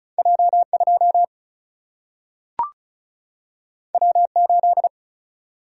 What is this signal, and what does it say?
Morse code, heard as a keyed tone, about 36 wpm, 700 Hz. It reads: J2 W8
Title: Single-Letter Single-Number 35wpm